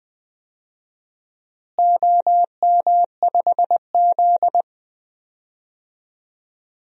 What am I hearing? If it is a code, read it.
Morse code, heard as a keyed tone, about 20 wpm, 700 Hz. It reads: OM5Z
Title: DXCC-100 Call Signs - 4 Characters - Repeated 3x - 20wpm